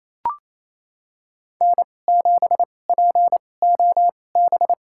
Morse code, heard as a keyed tone, about 28 wpm, spaced as for 20 wpm, 700 Hz. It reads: N7POB
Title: USA Call Signs - 5 Characters - 2x Character Spacing - 28wpm